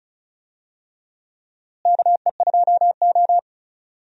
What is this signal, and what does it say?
Morse code, heard as a keyed tone, about 34 wpm, 700 Hz. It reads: KE2O
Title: USA Call Signs - 4 Characters - Repeated 2x - 35wpm